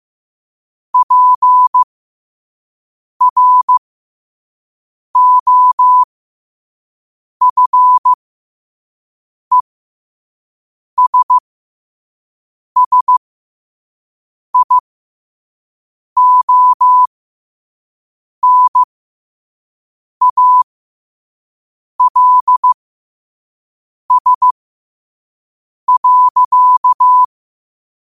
Morse code, heard as a keyed tone, about 15 wpm, spaced as for 5 wpm, 1000 Hz. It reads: PROFESSIONALS.
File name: qotdfiles / 5wpm